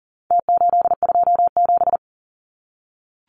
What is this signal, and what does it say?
Morse code, heard as a keyed tone, about 40 wpm, 700 Hz. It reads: T827
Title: Numbers - Contest Style - 4-Digit 40wpm